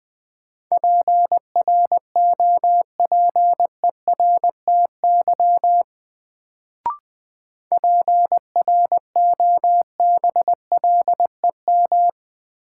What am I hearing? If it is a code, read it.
Morse code, heard as a keyed tone, about 20 wpm, 700 Hz. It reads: PROPERTY PROBLEM
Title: Sentences from Top 1000 Words - New Word Drill - 20wpm